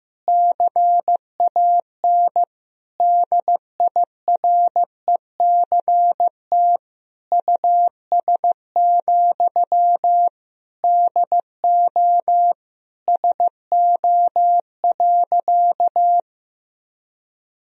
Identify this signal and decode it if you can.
Morse code, heard as a keyed tone, about 15 wpm, 700 Hz. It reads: CAN DIRECT US, DO SO.
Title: Sentences from Top 1000 Words - Review Part 2 - 15wpm